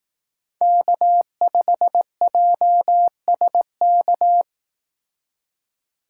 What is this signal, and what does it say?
Morse code, heard as a keyed tone, about 18 wpm, 700 Hz. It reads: K5JSK
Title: Speed Racing - USA 1x3 Call Signs 15wpm